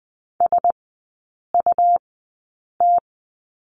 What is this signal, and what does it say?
Morse code, heard as a keyed tone, about 20 wpm, spaced as for 8 wpm, 700 Hz. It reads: SUT
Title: 100 Lesson 17e - Letter u - Three Characters 20wpm